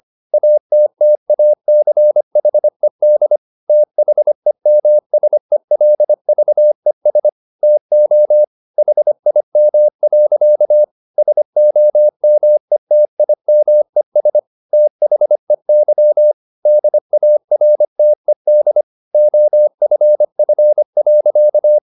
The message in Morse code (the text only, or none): ATTACHED THEMSELVES TO HIM. SOMETIMES THEY DARTED OFF.